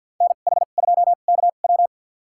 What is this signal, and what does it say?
Morse code, heard as a keyed tone, about 52 wpm, spaced as for 36 wpm, 700 Hz. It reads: NH2KW
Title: USA Call Signs - 5 Characters - 2x Character Spacing - 50wpm